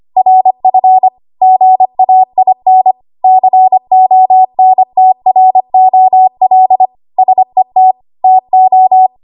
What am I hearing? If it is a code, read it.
Morse code, heard as a keyed tone, about 25 wpm, 750 Hz. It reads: RF GAIN CONTROL SET TO
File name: Archive / 25 WPM / 25WPM